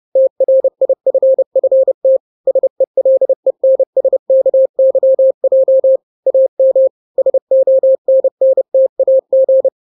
Morse code, heard as a keyed tone, about 30 wpm, 530 Hz. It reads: TRIFFT SELENSKYJ AM SONNTAG
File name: daily / 29wpm